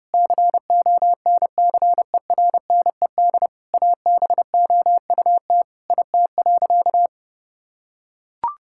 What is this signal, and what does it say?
Morse code, heard as a keyed tone, about 30 wpm, 700 Hz. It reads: CONCERNED ABOUT IT.
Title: Sentences from Top 1000 Words - Review Part 6 - 30wpm